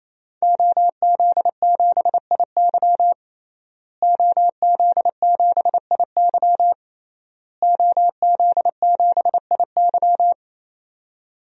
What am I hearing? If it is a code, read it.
Morse code, heard as a keyed tone, about 28 wpm, 700 Hz. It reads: OZ7IY OZ7IY OZ7IY
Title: DXCC-100 Call Signs - 5 Characters - Repeated 3x - 28wpm